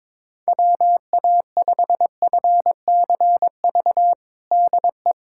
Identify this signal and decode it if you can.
Morse code, heard as a keyed tone, about 22 wpm, 700 Hz. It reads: WA5FCV DE
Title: USA Callsign DE Callsign - 6 Characters - 22wpm